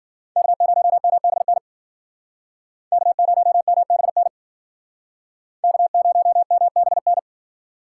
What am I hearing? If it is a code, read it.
Morse code, heard as a keyed tone, about 51 wpm, 700 Hz. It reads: K0MBN K0MBN K0MBN